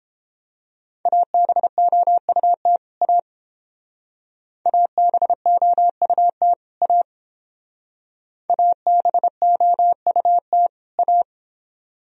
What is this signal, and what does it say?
Morse code, heard as a keyed tone, about 29 wpm, 700 Hz. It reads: ABOUT A ABOUT A ABOUT A